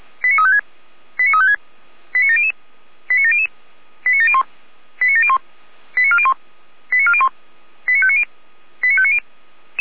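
Five-tone selective call (ZVEI), heard as quick runs of five tones, tones of about 70 ms. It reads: 89356 89356 89700 89700 89701 89701 89401 89401 89590 89590
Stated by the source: ZVEI five-tone selective call received with an SDR and recorded by Feuerwehr Orlamünde.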